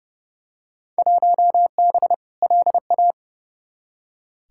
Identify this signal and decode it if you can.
Morse code, heard as a keyed tone, about 31 wpm, 700 Hz. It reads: 1B LA
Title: Field Day Exchanges - 30wpm